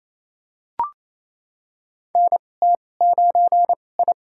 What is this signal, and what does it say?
Morse code, heard as a keyed tone, about 28 wpm, spaced as for 20 wpm, 700 Hz. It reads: NT9I
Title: USA Call Signs - 2x Character Spacing - 28wpm